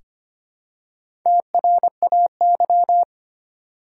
Morse code, heard as a keyed tone, about 26 wpm, 690 Hz. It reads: TRAY
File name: practice / 25wpm / Easy Words with Hints - 25wpm